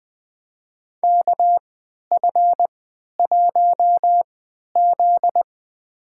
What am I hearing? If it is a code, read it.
Morse code, heard as a keyed tone, about 20 wpm, spaced as for 11 wpm, 700 Hz. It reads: KF1Z
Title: USA Call Signs - 3x Character Spacing - 20wpm